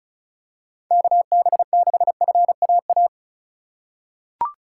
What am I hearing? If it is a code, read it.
Morse code, heard as a keyed tone, about 36 wpm, 700 Hz. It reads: KB6FAA
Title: USA Call Signs - 2x Character Spacing - 35wpm